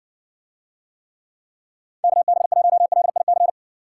Morse code, heard as a keyed tone, about 60 wpm, 700 Hz. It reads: KB1LIC